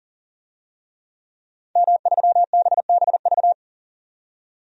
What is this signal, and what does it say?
Morse code, heard as a keyed tone, about 40 wpm, 700 Hz. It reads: M3BBV
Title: DXCC-100 Call Signs - 5 Characters - 40wpm